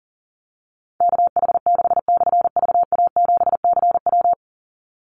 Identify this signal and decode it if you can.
Morse code, heard as a keyed tone, about 40 wpm, 700 Hz. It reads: KH6/VA7CW
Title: Call Signs - 40wpm